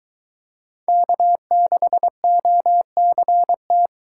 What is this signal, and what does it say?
Morse code, heard as a keyed tone, about 20 wpm, 700 Hz. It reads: K6OCT